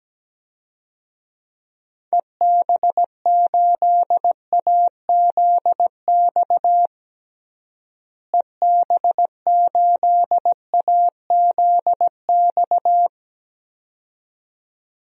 Morse code, heard as a keyed tone, about 17 wpm, 700 Hz. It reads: EB8AZX EB8AZX
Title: DXCC-100 Call Signs - 6 Characters - Repeated 2x - 17wpm